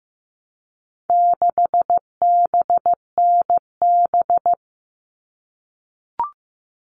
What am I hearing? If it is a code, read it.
Morse code, heard as a keyed tone, about 15 wpm, 700 Hz. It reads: 6BNB
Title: All Cut Numbers - 4-Digit 15wpm